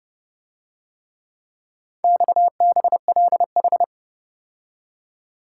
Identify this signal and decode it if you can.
Morse code, heard as a keyed tone, about 28 wpm, 700 Hz. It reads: XBLH